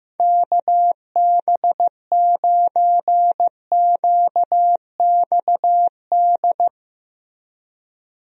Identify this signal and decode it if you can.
Morse code, heard as a keyed tone, about 15 wpm, 700 Hz. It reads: KB9QXD
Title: USA Call Signs - 6 Characters - Repeated 3x - 15wpm